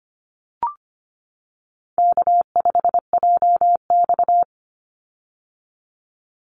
Morse code, heard as a keyed tone, about 25 wpm, 700 Hz. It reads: K5JX
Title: Call Signs - 25wpm